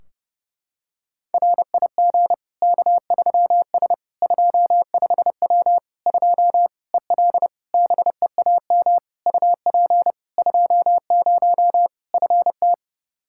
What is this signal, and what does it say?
Morse code, heard as a keyed tone, about 30 wpm, 700 Hz. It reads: RIG K3S 25W 2 EL BEAM UP 20 FT
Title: Practice QSO - Exchange 2 - Info - 30wpm